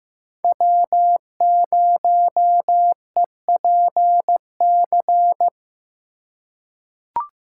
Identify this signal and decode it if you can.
Morse code, heard as a keyed tone, about 15 wpm, 700 Hz. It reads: W0EPC
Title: K1USN SST QSO Practice - 15wpm